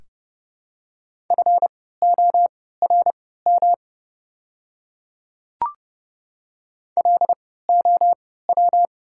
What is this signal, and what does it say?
Morse code, heard as a keyed tone, about 30 wpm, spaced as for 17 wpm, 700 Hz. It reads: FORM LOW
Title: 3x Character Spacing - Top 200 Words - 30wpm